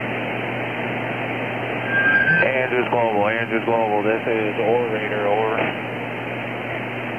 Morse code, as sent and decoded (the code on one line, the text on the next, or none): none